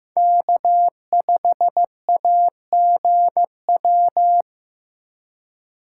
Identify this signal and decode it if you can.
Morse code, heard as a keyed tone, about 18 wpm, 700 Hz. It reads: K5AGW
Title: Speed Racing - USA Call Signs 15wpm